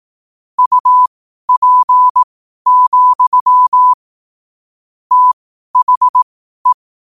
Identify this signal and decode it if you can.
Morse code, heard as a keyed tone, about 18 wpm, spaced as for 12 wpm, 1000 Hz. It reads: UP, THE
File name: qotdfiles / 12wpm